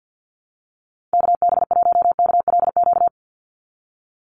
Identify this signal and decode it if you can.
Morse code, heard as a keyed tone, about 49 wpm, 700 Hz. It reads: KB1KLQ
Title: USA Call Signs 50wpm